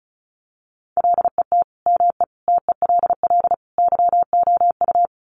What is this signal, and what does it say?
Morse code, heard as a keyed tone, about 34 wpm, 700 Hz. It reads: LET ME TELL YOU
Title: Sets of 4 Words 35wpm